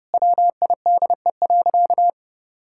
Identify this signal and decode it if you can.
Morse code, heard as a keyed tone, about 30 wpm, 700 Hz. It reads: WIDE.
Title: Sentences from Top 1000 Words - Review Part 6 - 30wpm